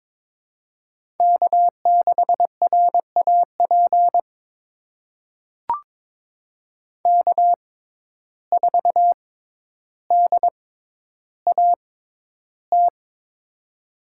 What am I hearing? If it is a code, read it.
Morse code, heard as a keyed tone, about 22 wpm, spaced as for 11 wpm, 700 Hz. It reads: K6RAP K4DAT
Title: USA Call Signs - 5 Characters - 6x Character Spacing - 22wpm